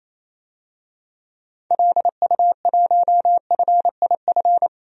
Morse code, heard as a keyed tone, about 28 wpm, 700 Hz. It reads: LU1FIF
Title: DXCC-100 Call Signs - 6 Characters - Repeated 2x - 28wpm